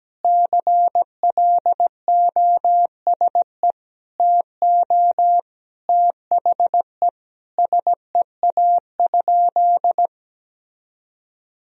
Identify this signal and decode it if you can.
Morse code, heard as a keyed tone, about 17 wpm, 700 Hz. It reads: CLOSE TO THE SEA?
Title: Sentences from Top 300 Words and Abbreviations 17wpm